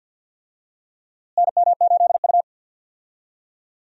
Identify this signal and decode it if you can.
Morse code, heard as a keyed tone, about 52 wpm, 700 Hz. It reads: NM8U
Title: USA Call Signs - 4 Characters - Repeated 2x - 50wpm